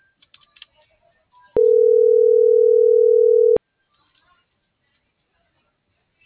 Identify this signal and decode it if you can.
Ringback tone (North America), 440 + 480 Hz.